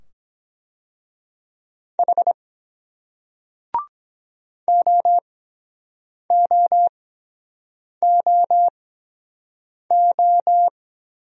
Morse code, heard as a keyed tone, about 23 wpm, spaced as for 7 wpm, 700 Hz. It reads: H OOOO